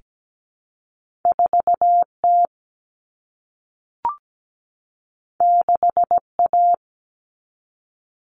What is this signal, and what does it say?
Morse code, heard as a keyed tone, about 17 wpm, 700 Hz. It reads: 4T 6A
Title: Common Cut Numbers - 2-Digit 17wpm